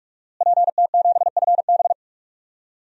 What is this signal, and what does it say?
Morse code, heard as a keyed tone, about 46 wpm, 700 Hz. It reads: PT7FB
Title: DXCC-100 Call Signs - 5 Characters - Repeated 2x - 45wpm